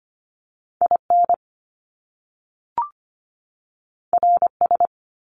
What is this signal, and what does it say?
Morse code, heard as a keyed tone, about 25 wpm, 700 Hz. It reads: I N R S
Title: Top Two Letter Combinations in Words 25wpm